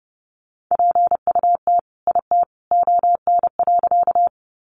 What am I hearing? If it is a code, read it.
Morse code, heard as a keyed tone, about 30 wpm, 700 Hz. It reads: PUT IT ON.
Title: Sentences from Top 200 Words 30wpm